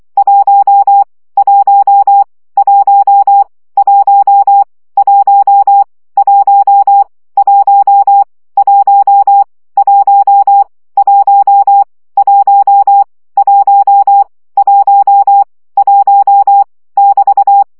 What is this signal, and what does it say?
Morse code, heard as a keyed tone, about 24 wpm, 800 Hz. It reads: 1 1 1 1 1 1 1 1 1 1 1 1 1 1 <BT>
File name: CW gyakorlas 1 20 baud